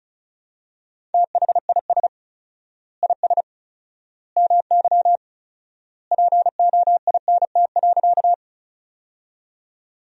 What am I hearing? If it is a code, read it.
Morse code, heard as a keyed tone, about 36 wpm, 700 Hz. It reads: THIS IS MY POINT.
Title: 4x Word-Spacing - Sentences from Top 200 Words - 35wpm